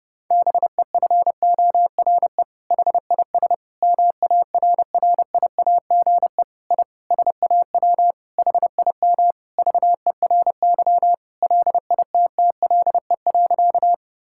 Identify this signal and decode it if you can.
Morse code, heard as a keyed tone, about 30 wpm, 700 Hz. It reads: BEFORE HIS MARRIAGE I SAW HIM VERY LITTLE.
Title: Sentences from Top 1000 Words - Review Part 5 - 30wpm